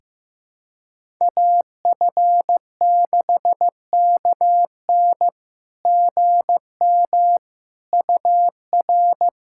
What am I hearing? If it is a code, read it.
Morse code, heard as a keyed tone, about 16 wpm, 700 Hz. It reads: AF6KN GM UR
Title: SOTA QSO Practice 15wpm